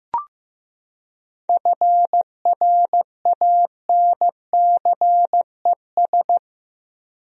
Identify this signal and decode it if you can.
Morse code, heard as a keyed tone, about 15 wpm, 700 Hz. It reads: FRANCES
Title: USA Common Names - 15wpm